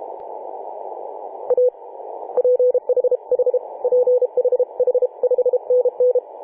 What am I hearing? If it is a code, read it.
Morse code, heard as a keyed tone, about 32 wpm, 500 Hz. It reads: A PH H PH H 5NN
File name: 2013/Sep/21 2035z 3D2GC 18.072MHz CW